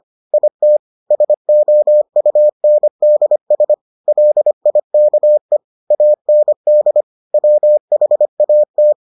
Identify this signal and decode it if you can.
Morse code, heard as a keyed tone, about 25 wpm, 600 Hz. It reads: IT SOUNDS LIKE AND WHAT